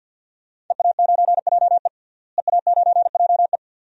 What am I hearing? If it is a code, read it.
Morse code, heard as a keyed tone, about 46 wpm, 700 Hz. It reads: EA9JE EA9JE
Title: DXCC-100 Call Signs - 5 Characters - Repeated 2x - 50wpm